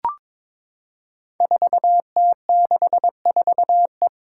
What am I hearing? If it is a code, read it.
Morse code, heard as a keyed tone, about 22 wpm, 700 Hz. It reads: 4T64E
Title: Five Challenging Letter-Number Combinations 22wpm